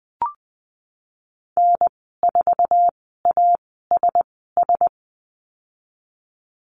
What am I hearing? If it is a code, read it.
Morse code, heard as a keyed tone, about 20 wpm, spaced as for 15 wpm, 700 Hz. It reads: N4ASS
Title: 048 Lesson 9g - Number 4 - Call Signs 20wpm